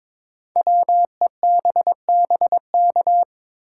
Morse code, heard as a keyed tone, about 23 wpm, 700 Hz. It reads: WEBBK
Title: Code Groups - 5 Letter:Numbers - 22wpm